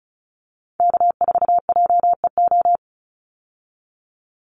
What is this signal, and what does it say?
Morse code, heard as a keyed tone, about 34 wpm, 700 Hz. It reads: K4JEO
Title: USA Call Signs 35wpm